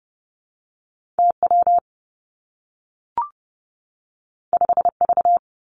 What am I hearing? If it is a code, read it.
Morse code, heard as a keyed tone, about 30 wpm, 700 Hz. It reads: T W 5 V
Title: Two Challenging Letter-Number Combinations 30wpm